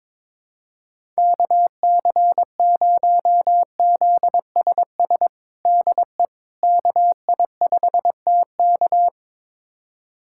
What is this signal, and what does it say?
Morse code, heard as a keyed tone, about 22 wpm, 700 Hz. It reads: KC0ZSS DE KI5TK
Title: USA Callsign DE Callsign 22wpm